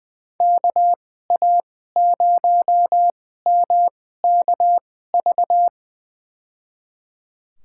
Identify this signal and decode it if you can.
Morse code, heard as a keyed tone, about 20 wpm, spaced as for 14 wpm, 700 Hz. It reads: KA0MKV
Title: USA Call Signs - 6 Characters - 2x Character Spacing - 20wpm